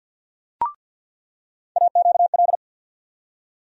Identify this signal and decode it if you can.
Morse code, heard as a keyed tone, about 47 wpm, 700 Hz. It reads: AQL
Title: USA Call Sign Postfixes - 3 Letters 50wpm